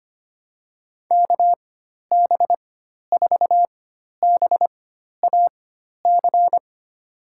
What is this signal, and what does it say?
Morse code, heard as a keyed tone, about 25 wpm, spaced as for 12 wpm, 700 Hz. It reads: KB4BAC
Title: USA Call Signs - 6 Characters - 4x Character Spacing - 25wpm